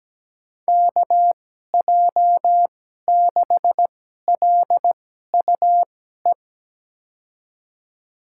Morse code, heard as a keyed tone, about 17 wpm, spaced as for 12 wpm, 700 Hz. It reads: KJ6LUE
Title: USA Call Signs - 6 Characters - 2x Character Spacing - 17wpm